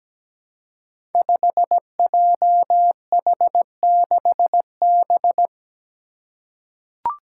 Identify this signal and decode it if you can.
Morse code, heard as a keyed tone, about 17 wpm, 700 Hz. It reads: 5JH6B
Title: Five Challenging Letter-Number Combinations 17wpm